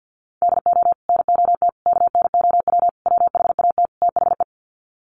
Morse code, heard as a keyed tone, about 50 wpm, 700 Hz. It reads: DO NOT KNOW WHAT THE
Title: Sets of 5 Words 50wpm